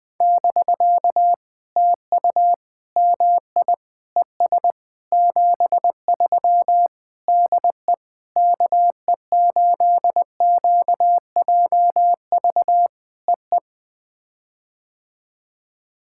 Morse code, heard as a keyed tone, about 19 wpm, 700 Hz. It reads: <BK> TU MI ES 73 DE KE8QJV EE
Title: POTA QSO Practice - 20wpm